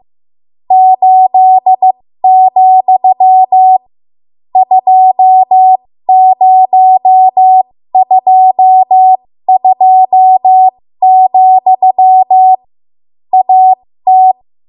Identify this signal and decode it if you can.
Morse code, heard as a keyed tone, about 15 wpm, spaced as for 13 wpm, 750 Hz. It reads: 8, 2022, AT
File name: Morse / 13WPM